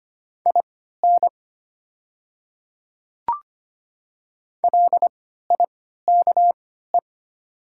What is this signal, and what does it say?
Morse code, heard as a keyed tone, about 25 wpm, spaced as for 14 wpm, 700 Hz. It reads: IN LIKE
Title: 3x Character Spacing - Top 100 Words - 25wpm